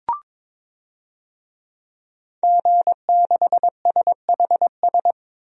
Morse code, heard as a keyed tone, about 22 wpm, 700 Hz. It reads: G6SHS
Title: DXCC-100 Call Signs - 5 Characters - Repeated 3x - 22wpm